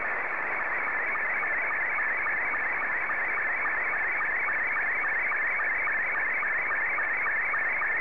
SSTV (slow-scan television), Scottie 2.